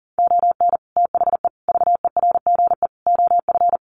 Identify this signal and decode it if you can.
Morse code, heard as a keyed tone, about 40 wpm, 700 Hz. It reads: ON THE VERGE OF
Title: Sets of 4 Words 40wpm